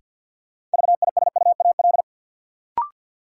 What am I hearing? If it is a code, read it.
Morse code, heard as a keyed tone, about 47 wpm, 700 Hz. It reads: VISUAL
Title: Sets of 5 Words Encore - New Word Drill - 50wpm